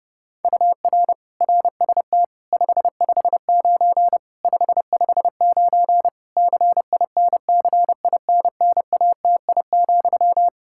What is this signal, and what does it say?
Morse code, heard as a keyed tone, about 30 wpm, 700 Hz. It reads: UR RST 559 559 CINCINNATI,
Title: Practice QSO - Exchange 1 - Beginning - 30wpm